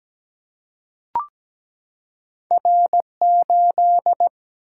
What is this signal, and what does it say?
Morse code, heard as a keyed tone, about 17 wpm, 700 Hz. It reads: R8
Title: Single-Letter Single-Number 17wpm